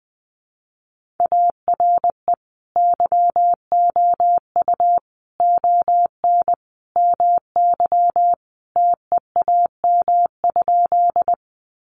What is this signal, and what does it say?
Morse code, heard as a keyed tone, about 20 wpm, 700 Hz. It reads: ARE YOU ON MY TEAM?
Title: Sentences from Top 1500 Words 20wpm